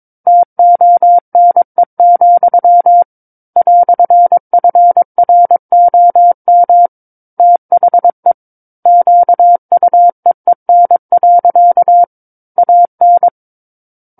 Morse code, heard as a keyed tone, about 22 wpm, 700 Hz. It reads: TONE, "FROM THE QUEEN. AN
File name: 06-Chapter-06 22 WPM